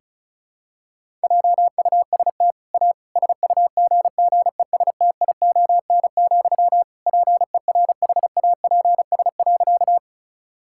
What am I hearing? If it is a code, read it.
Morse code, heard as a keyed tone, about 36 wpm, 700 Hz. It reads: JUST A SUGGESTION, PERHAPS.
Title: Sentences from Top 1600 Words 35wpm